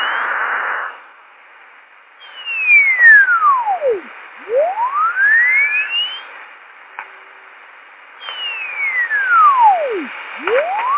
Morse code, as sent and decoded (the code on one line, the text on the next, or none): none